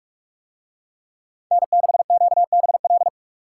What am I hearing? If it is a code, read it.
Morse code, heard as a keyed tone, about 44 wpm, 700 Hz. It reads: N6QBL